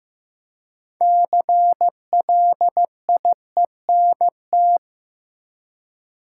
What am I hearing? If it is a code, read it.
Morse code, heard as a keyed tone, about 15 wpm, 700 Hz. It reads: CLIENT
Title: Six Letter Words - 15wpm